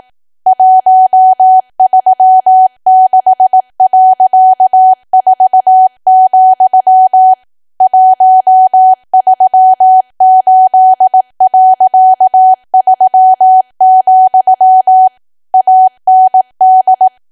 Morse code, heard as a keyed tone, about 18 wpm, 750 Hz. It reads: 136.4, 138.3, AND
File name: cwa / W1AW-Level2 / 18WPM